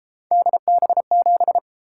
Morse code, heard as a keyed tone, about 33 wpm, 700 Hz. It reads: DB7